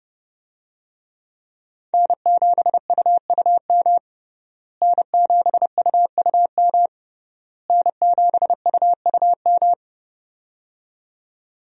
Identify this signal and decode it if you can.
Morse code, heard as a keyed tone, about 30 wpm, 700 Hz. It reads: N7UUM N7UUM N7UUM
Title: USA Call Signs - 5 Characters - Repeated 3x - 30wpm